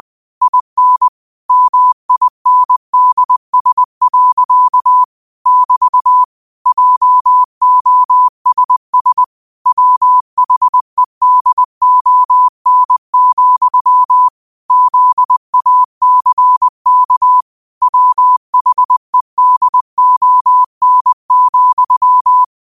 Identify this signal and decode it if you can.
Morse code, heard as a keyed tone, about 20 wpm, 1000 Hz. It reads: IN MINDS. <BT> JOSS WHEDON, ZACK WHEDON,
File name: qotdfiles / 20wpm